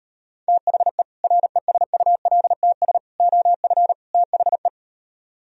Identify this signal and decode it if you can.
Morse code, heard as a keyed tone, about 38 wpm, 700 Hz. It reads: THE RESULTS OF THE